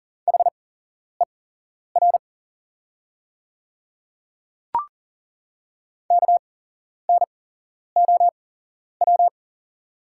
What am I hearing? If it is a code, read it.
Morse code, heard as a keyed tone, about 40 wpm, spaced as for 11 wpm, 700 Hz. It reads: HER KNOW
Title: 8x Character Spacing - Top 200 Words - 40wpm